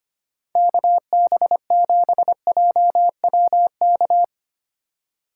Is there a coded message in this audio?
Yes: Morse code from 0.6 to 4.2 s, heard as a keyed tone, about 33 wpm, 700 Hz. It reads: KB7JWK